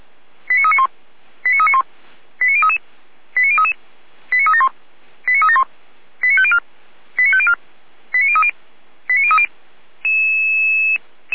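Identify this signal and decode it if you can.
Five-tone selective call (ZVEI), heard as quick runs of five tones, tones of about 70 ms. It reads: 89391 89391 89033 89033 89371 89371 89504 89504 89030 89030
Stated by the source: ZVEI five-tone call 89125 received by SDR, archive path 455.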